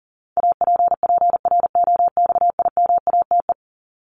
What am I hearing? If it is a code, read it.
Morse code, heard as a keyed tone, about 40 wpm, 700 Hz. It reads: APPROXIMATE
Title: LY Suffix - Top 100 Words with Base Words 40wpm